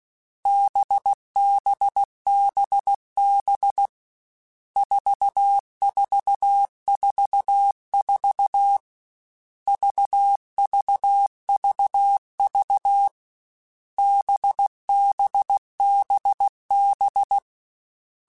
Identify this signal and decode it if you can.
Morse code, heard as a keyed tone, about 16 wpm, 800 Hz. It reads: BBBB 4444 VVVV BBBB
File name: Groups@13 / Groups@13 mp3s / VB46@13